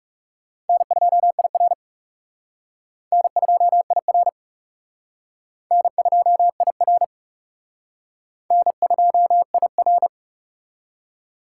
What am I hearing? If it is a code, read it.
Morse code, heard as a keyed tone, about 37 wpm, 700 Hz. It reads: N2IR N2IR N2IR N2IR
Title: Speed Racing - USA Call Signs 30wpm